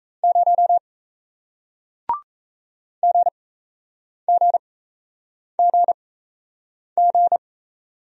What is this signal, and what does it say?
Morse code, heard as a keyed tone, about 37 wpm, spaced as for 8 wpm, 700 Hz. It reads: 0 GGGG